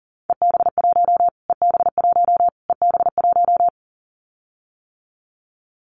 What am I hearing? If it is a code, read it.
Morse code, heard as a keyed tone, about 40 wpm, 700 Hz. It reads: EB1 EB1 EB1
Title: Callsign Prefixes for Top 25 DXCC - Repeated 3x - 40wpm